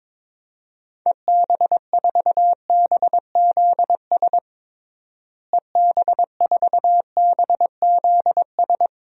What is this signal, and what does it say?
Morse code, heard as a keyed tone, about 22 wpm, 700 Hz. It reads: EB4BZS EB4BZS
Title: DXCC-100 Call Signs - 6 Characters - Repeated 3x - 22wpm